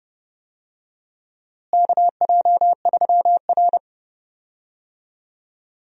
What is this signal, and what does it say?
Morse code, heard as a keyed tone, about 31 wpm, 700 Hz. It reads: KJ3R
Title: USA Call Signs - 4 Characters - Repeated 3x - 30wpm